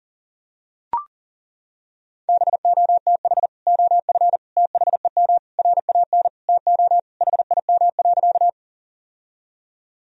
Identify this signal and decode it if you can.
Morse code, heard as a keyed tone, about 40 wpm, 700 Hz. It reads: BOTH OF THEM RAN TO HIM.
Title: Sentences from Top 1000 Words - Review Part 3 - 40wpm